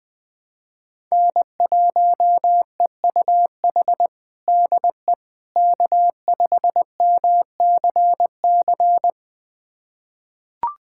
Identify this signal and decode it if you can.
Morse code, heard as a keyed tone, about 20 wpm, 700 Hz. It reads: N1EUH DE K5MCC
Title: USA Callsign DE Callsign - 5 Characters - 20wpm